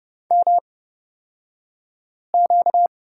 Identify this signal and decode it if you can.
Morse code, heard as a keyed tone, about 30 wpm, spaced as for 5 wpm, 700 Hz. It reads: MQ
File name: Lesson 38d - Letter q - Two Characters 30wpm